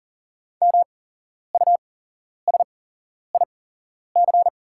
Morse code, heard as a keyed tone, about 40 wpm, spaced as for 11 wpm, 700 Hz. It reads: MUSIC
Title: 8x Character Spacing - Top 300 Words - 40wpm